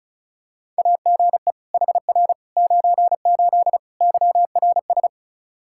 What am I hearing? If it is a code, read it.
Morse code, heard as a keyed tone, about 36 wpm, 700 Hz. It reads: AGE HR 98 YRS
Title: K1USN SST QSO Practice 35wpm